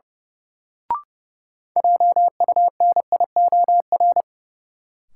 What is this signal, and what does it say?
Morse code, heard as a keyed tone, about 29 wpm, 700 Hz. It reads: JUNIOR
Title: Sentences from Top 1000 Words - Review Part 7 - New Word Drill - 30wpm